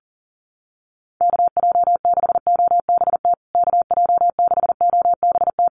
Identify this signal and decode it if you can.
Morse code, heard as a keyed tone, about 40 wpm, 700 Hz. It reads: KJ6OBT KJ6OBT
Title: USA Call Signs Repeated 3x 40wpm